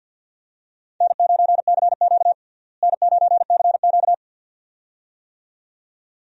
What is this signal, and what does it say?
Morse code, heard as a keyed tone, about 49 wpm, 700 Hz. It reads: N9CQ N9CQ
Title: USA Call Signs - 4 Characters - Repeated 2x - 50wpm